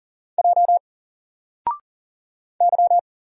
Morse code, heard as a keyed tone, about 40 wpm, 700 Hz. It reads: J Y
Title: Single Letter-Number 40wpm